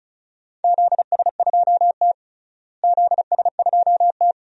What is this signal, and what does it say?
Morse code, heard as a keyed tone, about 34 wpm, 700 Hz. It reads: ZS2T ZS2T
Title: DXCC-100 Call Signs - 4 Characters - Repeated 3x - 35wpm